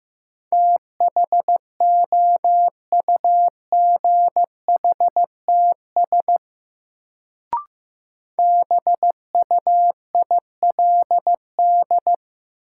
Morse code, heard as a keyed tone, about 15 wpm, 700 Hz. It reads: THOUGHTS BUILD
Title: Sentences from Top 1000 Words - Review Part 1 - New Word Drill - 15wpm